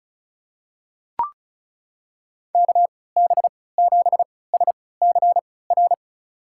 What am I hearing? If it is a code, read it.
Morse code, heard as a keyed tone, about 36 wpm, spaced as for 20 wpm, 700 Hz. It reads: KB7SCR
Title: USA Call Signs - 3x Character Spacing - 35wpm